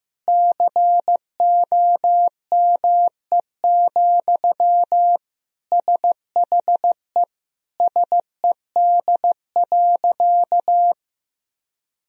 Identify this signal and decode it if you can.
Morse code, heard as a keyed tone, about 15 wpm, 700 Hz. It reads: COME, SHE SED.
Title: Sentences from Top 100 Words and Abbreviations 15wpm